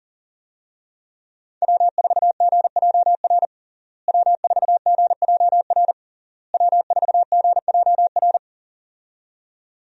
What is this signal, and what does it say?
Morse code, heard as a keyed tone, about 40 wpm, 700 Hz. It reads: W4GJR W4GJR W4GJR
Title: USA Call Signs - 5 Characters - Repeated 3x - 40wpm